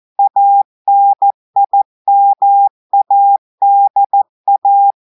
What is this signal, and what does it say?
Morse code, heard as a keyed tone, about 14 wpm, 800 Hz. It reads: ANIMADA
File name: audios / portugais / portugais texte en clair 1 vitesse 14 mots minutes